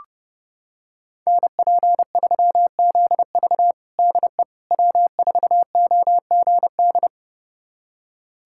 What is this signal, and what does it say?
Morse code, heard as a keyed tone, about 32 wpm, 700 Hz. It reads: NP3ZV DE W4OGD
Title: USA Callsign DE Callsign - 5 Characters - 30wpm